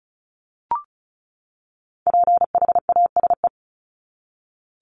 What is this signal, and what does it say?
Morse code, heard as a keyed tone, about 34 wpm, 700 Hz. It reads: PHASE
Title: Top 1001-1250 Words 35wpm